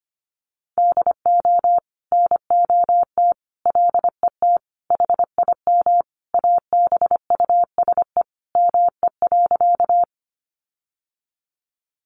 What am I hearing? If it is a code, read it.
Morse code, heard as a keyed tone, about 25 wpm, 700 Hz. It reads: DO NOT LET HIM ABUSE ME.
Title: Sentences from Top 1300 Words 25wpm